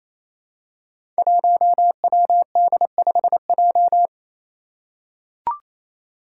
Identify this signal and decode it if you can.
Morse code, heard as a keyed tone, about 27 wpm, 700 Hz. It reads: 1WD5J
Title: Five Challenging Letter-Number Combinations 28wpm